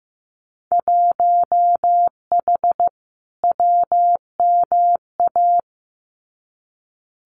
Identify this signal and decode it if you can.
Morse code, heard as a keyed tone, about 15 wpm, 700 Hz. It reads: 1H WMA
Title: Winter Field Day Exchanges Encore 15wpm